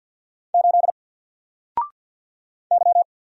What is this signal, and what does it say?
Morse code, heard as a keyed tone, about 48 wpm, 700 Hz. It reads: 8 Y
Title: Single Letter-Number Alt 50wpm